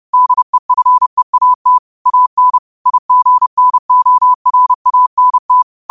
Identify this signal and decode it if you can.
Morse code, heard as a keyed tone, about 30 wpm, 1000 Hz. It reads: DEFEAT AN IGNORANT